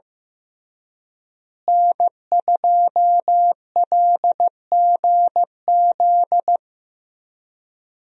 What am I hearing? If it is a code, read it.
Morse code, heard as a keyed tone, about 15 wpm, 700 Hz. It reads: N2LGZ
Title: USA Call Signs - 5 Characters - Repeated 3x - 15wpm